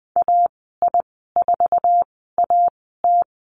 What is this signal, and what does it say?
Morse code, heard as a keyed tone, about 20 wpm, spaced as for 14 wpm, 700 Hz. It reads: AI4AT
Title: 069 Lesson 12g - Letter d - Call Signs 20wpm